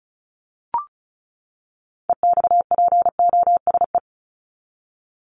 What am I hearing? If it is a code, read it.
Morse code, heard as a keyed tone, about 34 wpm, 700 Hz. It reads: EXPOSE
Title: Top 1501-1750 Words 35wpm